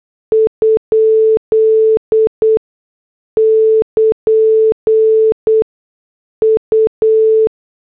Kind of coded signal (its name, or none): Morse code